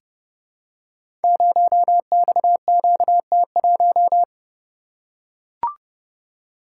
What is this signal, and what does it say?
Morse code, heard as a keyed tone, about 30 wpm, 700 Hz. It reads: 0XQT1
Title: Code Groups - 5 Letter:Numbers - 30wpm